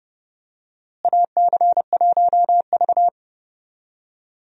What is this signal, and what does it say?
Morse code, heard as a keyed tone, about 32 wpm, 700 Hz. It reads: AC1V
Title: USA Call Signs - 4 Characters - Repeated 3x - 30wpm